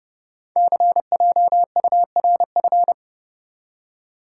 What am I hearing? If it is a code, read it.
Morse code, heard as a keyed tone, about 30 wpm, 690 Hz. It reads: CJURF
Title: Code Groups - 5 Letters - 30wpm